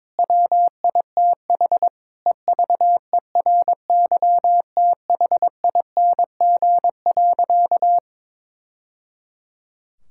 Morse code, heard as a keyed tone, about 22 wpm, 700 Hz. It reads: WITH EVERYTHING.
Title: Sentences from Top 1000 Words - Review Part 5 - 22wpm